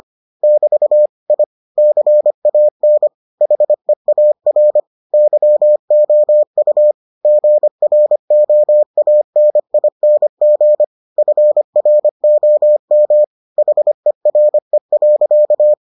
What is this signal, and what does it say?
Morse code, heard as a keyed tone, about 25 wpm, 600 Hz. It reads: <BT> I CAN HEAR YOU GROANING FROM HERE.